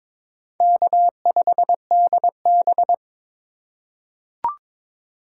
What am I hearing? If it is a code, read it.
Morse code, heard as a keyed tone, about 22 wpm, 700 Hz. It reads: K5DB
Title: USA Call Signs - 4 Characters - 2x Character Spacing - 22wpm